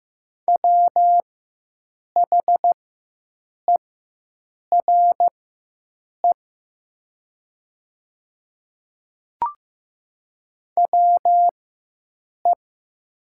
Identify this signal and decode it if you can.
Morse code, heard as a keyed tone, about 15 wpm, spaced as for 7 wpm, 700 Hz. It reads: WHERE WE
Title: 4x Character Spacing - Top 500 Words - 15wpm